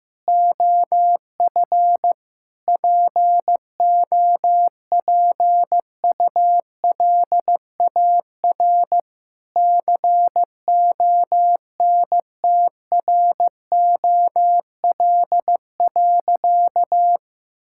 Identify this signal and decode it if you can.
Morse code, heard as a keyed tone, about 15 wpm, 700 Hz. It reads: OF POPULAR CONTROL.
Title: Sentences from Top 1000 Words - Review Part 5 - 15wpm